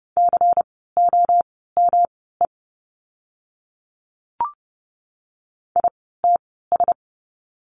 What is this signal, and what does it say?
Morse code, heard as a keyed tone, about 30 wpm, spaced as for 17 wpm, 700 Hz. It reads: COME ITS
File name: course / 30:10wpm / 115 Lesson 19f - Letter w - Words 30wpm